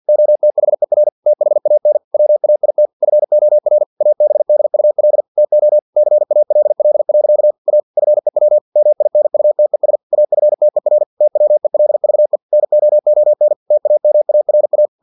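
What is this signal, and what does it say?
Morse code, heard as a keyed tone, about 50 wpm, 600 Hz. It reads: OTHER THAN WAIT FOR ABDUL TO CALL. A FEW MINUTES AFTER TWELVE NOON TAMARA
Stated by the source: Morse practice text from an English-language novel at 49 wpm.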